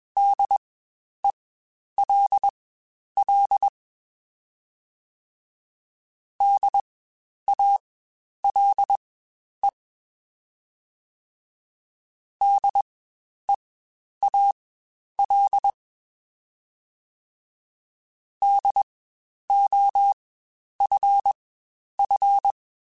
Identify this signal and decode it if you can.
Morse code, heard as a keyed tone, about 21 wpm, spaced as for 10 wpm, 800 Hz. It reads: DELL DALE DEAL DOFF
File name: Spaced Alphas / Spaced Alphas mp3s / Ds sp @20